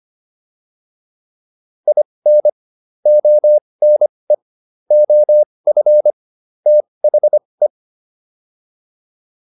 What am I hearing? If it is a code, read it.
Morse code, heard as a keyed tone, about 25 wpm, spaced as for 20 wpm, 600 Hz. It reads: IN ONE OF THE